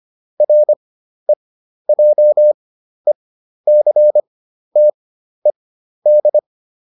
Morse code, heard as a keyed tone, about 25 wpm, spaced as for 12 wpm, 600 Hz. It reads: REJECTED